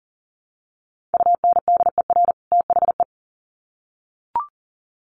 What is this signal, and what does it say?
Morse code, heard as a keyed tone, about 40 wpm, 700 Hz. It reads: UNDER THE
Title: Sets of 2 Words 40wpm